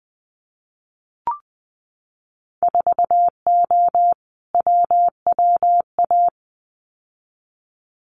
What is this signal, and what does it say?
Morse code, heard as a keyed tone, about 20 wpm, 700 Hz. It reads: 4O WWA
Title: Winter Field Day Exchanges 20wpm